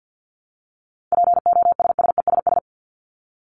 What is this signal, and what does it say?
Morse code, heard as a keyed tone, about 50 wpm, 700 Hz. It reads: POSSESS
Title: Top 1751-2000 Words 50wpm